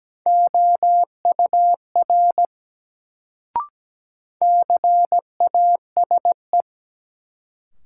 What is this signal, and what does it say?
Morse code, heard as a keyed tone, about 17 wpm, 700 Hz. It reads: OUR CASE
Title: Sets of 2 Words - New Word Drill - 17wpm